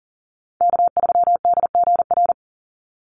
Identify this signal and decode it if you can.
Morse code, heard as a keyed tone, about 40 wpm, 700 Hz. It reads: K3DGR
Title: Call Signs - 40wpm